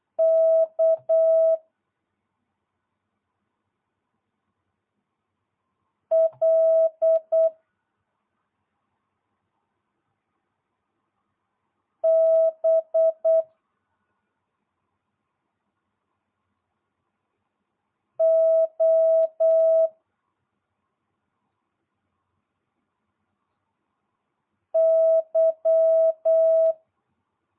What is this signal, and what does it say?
Morse code, heard as a keyed tone, about 8 wpm, 650 Hz. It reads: K L B O Y